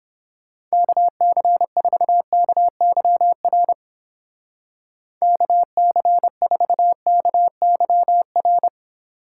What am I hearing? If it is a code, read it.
Morse code, heard as a keyed tone, about 29 wpm, 700 Hz. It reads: KC4KYR KC4KYR